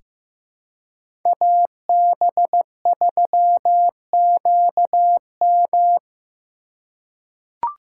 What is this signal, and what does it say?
Morse code, heard as a keyed tone, about 15 wpm, 700 Hz. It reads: AB3QM
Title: USA Call Signs - 5 Characters - 4x Character Spacing - 15wpm